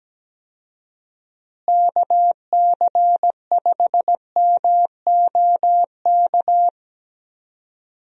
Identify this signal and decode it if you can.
Morse code, heard as a keyed tone, about 17 wpm, 700 Hz. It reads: KC5MOK
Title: USA Call Signs - 6 Characters - Repeated 3x - 17wpm